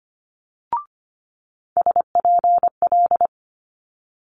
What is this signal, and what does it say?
Morse code, heard as a keyed tone, about 25 wpm, 700 Hz. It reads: SPL
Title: Multi-letter Phonemes Quick Fire 25wpm